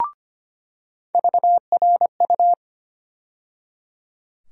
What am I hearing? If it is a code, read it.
Morse code, heard as a keyed tone, about 25 wpm, 700 Hz. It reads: VRU
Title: USA Call Sign Postfixes - 3 Letters 25wpm